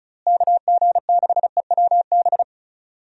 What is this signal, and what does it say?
Morse code, heard as a keyed tone, about 34 wpm, 700 Hz. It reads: KG6EWB
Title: USA Call Signs - 6 Characters - Repeated 3x - 35wpm